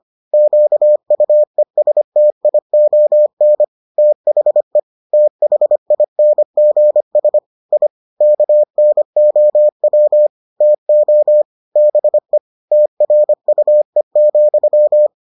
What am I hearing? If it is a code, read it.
Morse code, heard as a keyed tone, about 25 wpm, 600 Hz. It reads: QUESTION THE THINGS I KNOW TO BE TRUE,